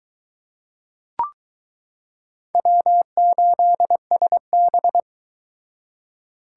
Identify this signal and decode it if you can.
Morse code, heard as a keyed tone, about 22 wpm, 700 Hz. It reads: W8SB